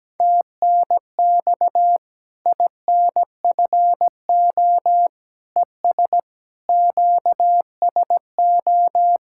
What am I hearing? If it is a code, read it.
Morse code, heard as a keyed tone, about 17 wpm, 700 Hz. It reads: TNX INFO ES QSO
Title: Practice QSO - Exchange 4 - Ending - 17wpm